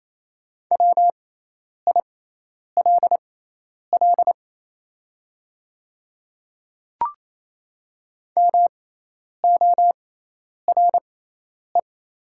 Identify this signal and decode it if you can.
Morse code, heard as a keyed tone, about 28 wpm, spaced as for 10 wpm, 690 Hz. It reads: WILL MORE
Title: 6x Character Spacing - Top 100 Words - 28wpm